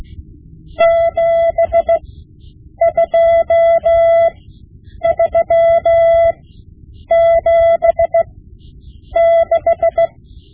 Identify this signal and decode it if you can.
Morse code, heard as a keyed tone, about 17 wpm, spaced as for 8 wpm, 670 Hz. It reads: 72376
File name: Morse Lessons / Morse Lessons / Tape-5B